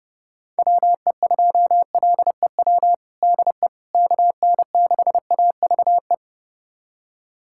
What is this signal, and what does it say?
Morse code, heard as a keyed tone, about 30 wpm, 700 Hz. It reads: WE2LEW DE KN6AVE
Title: USA Callsign DE Callsign - 6 Characters - 30wpm